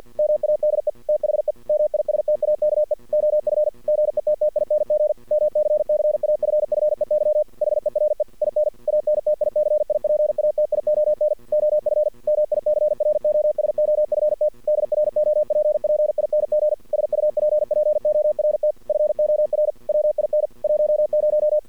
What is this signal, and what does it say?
Morse code, heard as a keyed tone, about 48 wpm, 600 Hz. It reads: MADE THE MISTAKE OF GETTING MCCAFFEY HERE IN ANTICIPATION OF MICROSOFT DROPPING SUPPORT FOR WIN 10
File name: archived / CRSnet